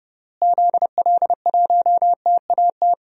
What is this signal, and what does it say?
Morse code, heard as a keyed tone, about 30 wpm, 700 Hz. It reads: ZL1TAT
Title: DXCC-100 Call Signs - 6 Characters - Repeated 3x - 30wpm